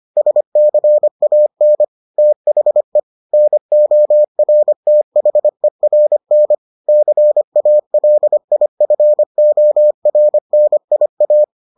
Morse code, heard as a keyed tone, about 25 wpm, 600 Hz. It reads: SCAN THE NORTHERN CALIFORNIA